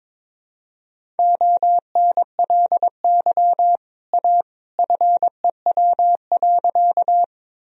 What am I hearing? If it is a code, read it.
Morse code, heard as a keyed tone, about 22 wpm, 700 Hz. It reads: ONLY A FEW.
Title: Sentences from Top 1000 Words - Review Part 1 - 22wpm